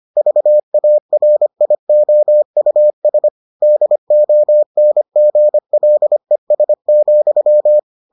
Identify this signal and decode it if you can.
Morse code, heard as a keyed tone, about 25 wpm, 600 Hz. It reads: VARIOUS DONGLES,